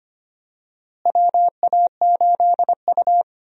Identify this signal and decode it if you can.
Morse code, heard as a keyed tone, about 25 wpm, 700 Hz. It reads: WA8U
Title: USA Call Signs - 4 Characters - Repeated 2x - 25wpm